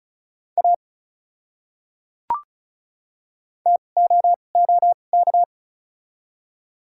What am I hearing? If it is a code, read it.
Morse code, heard as a keyed tone, about 36 wpm, spaced as for 25 wpm, 700 Hz. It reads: A TOOK
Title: 2x Character Spacing - Top 300 Words - 35wpm